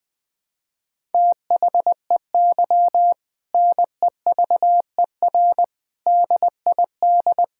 Morse code, heard as a keyed tone, about 21 wpm, 700 Hz. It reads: THEY NEVER DID
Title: Sentences from Top 1000 Words - Review Part 5 - 20wpm